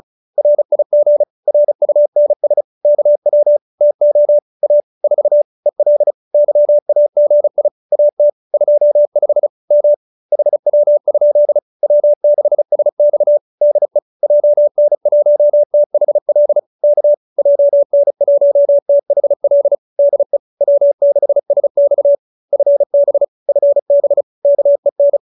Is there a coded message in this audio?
Yes: Morse code from 0.4 to 25.2 s, heard as a keyed tone, about 34 wpm, 590 Hz. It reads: RIG RUNS KW TO A 4 EL YAGI AT 25 M HW? W6SX DE JN1THL K JN1THL DE W6SX FB FB KEN